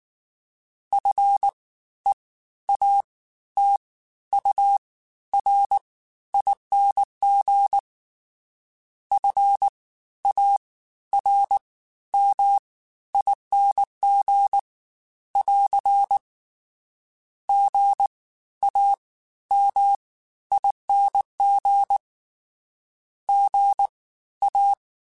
Morse code, heard as a keyed tone, about 19 wpm, spaced as for 12 wpm, 800 Hz. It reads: FEATURING FARMING <AR> GAMING GA
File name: Special INGs-mp3s / Special INGs-mp3s / Fast ing @18